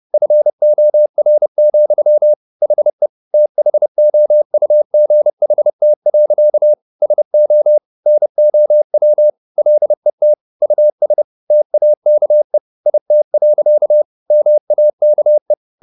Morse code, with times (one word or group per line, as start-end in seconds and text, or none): 0.1-2.3 FOR,
2.6-3.1 HE
3.3-6.7 THOUGHT.
7.0-7.8 SO
8.1-9.3 NOW
9.6-10.3 LET
10.6-11.2 US
11.5-12.6 TAKE
12.9-14.0 IT.
14.3-15.5 MAKE